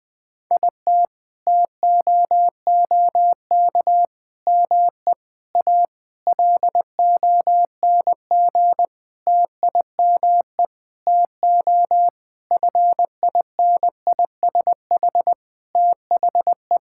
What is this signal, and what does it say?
Morse code, heard as a keyed tone, about 20 wpm, 700 Hz. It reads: IT TOOK ME A LONG TIME TO FINISH THE
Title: Sentences with 7-Word Phrases - Spoken Beginning - 20wpm